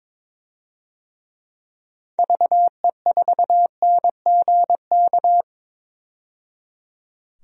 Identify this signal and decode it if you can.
Morse code, heard as a keyed tone, about 22 wpm, 700 Hz. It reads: VE4NGK
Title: DXCC-100 Call Signs - 6 Characters - 22wpm